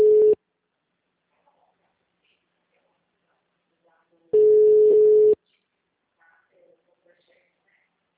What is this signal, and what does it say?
Ringback tone (Europe, 425 Hz).